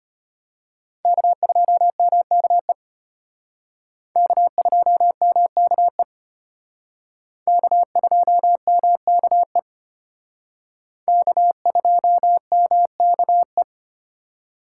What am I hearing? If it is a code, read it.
Morse code, heard as a keyed tone, about 32 wpm, 700 Hz. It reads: K2MKE K2MKE K2MKE K2MKE